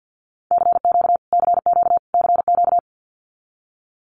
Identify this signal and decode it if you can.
Morse code, heard as a keyed tone, about 50 wpm, 700 Hz. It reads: CQ CQ CQ
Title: Abbreviations Repeated 3x - 50wpm